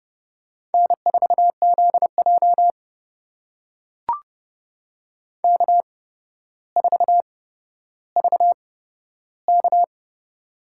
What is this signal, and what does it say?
Morse code, heard as a keyed tone, about 31 wpm, spaced as for 13 wpm, 700 Hz. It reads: N4ZJ K4VK
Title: USA Call Signs - 4 Characters - 8x Character Spacing - 30wpm